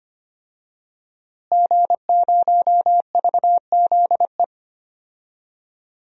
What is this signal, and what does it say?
Morse code, heard as a keyed tone, about 25 wpm, 700 Hz. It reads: G0VZE
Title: DXCC-100 Call Signs - 5 Characters - Repeated 3x - 25wpm